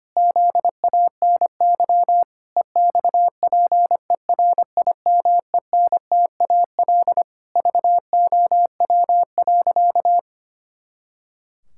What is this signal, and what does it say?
Morse code, heard as a keyed tone, about 25 wpm, 700 Hz. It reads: ZANY EXPERIMENTAL VOW.
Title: Pangram Challenge v2 - 25wpm